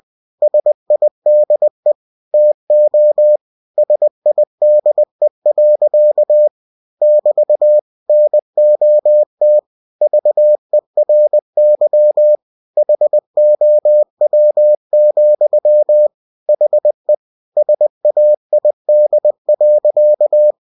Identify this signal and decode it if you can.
Morse code, heard as a keyed tone, about 20 wpm, 600 Hz. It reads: SIDE TO SIDE. <BT> NOT VERY HOW, HE SAID.
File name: cw-books / winnie-the-pooh / winnie-20wpm-page-0026